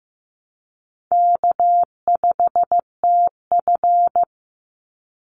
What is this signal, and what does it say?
Morse code, heard as a keyed tone, about 15 wpm, 700 Hz. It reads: K5TF
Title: Call Signs Repeated 2x - 15wpm